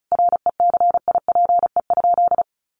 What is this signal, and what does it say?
Morse code, heard as a keyed tone, about 34 wpm, 700 Hz. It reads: RECIPE?
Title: Sentences from Top 1200 Words 35wpm